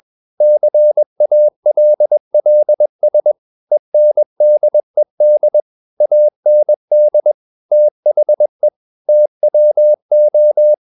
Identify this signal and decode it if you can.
Morse code, heard as a keyed tone, about 21 wpm, 600 Hz. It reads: CALLS ENDED AND THE TWO
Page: CW med Gnister